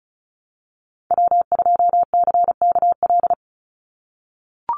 Morse code, heard as a keyed tone, about 34 wpm, 700 Hz. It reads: W2CKL
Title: NA SOTA Activator Callsigns 35wpm